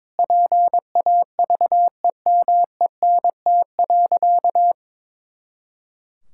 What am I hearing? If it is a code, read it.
Morse code, heard as a keyed tone, about 21 wpm, 700 Hz. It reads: PAVEMENT.
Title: Pangram Challenge v2 - 22wpm